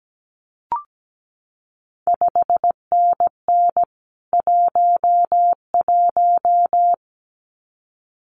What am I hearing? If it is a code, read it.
Morse code, heard as a keyed tone, about 17 wpm, 700 Hz. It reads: 5NN 11
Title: CQ WW DX Contest Exchange Encore - 17wpm